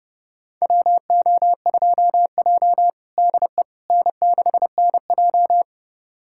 Morse code, heard as a keyed tone, about 32 wpm, 700 Hz. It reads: WO2J DE N6NJ
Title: USA Callsign DE Callsign - 4 Characters - 30wpm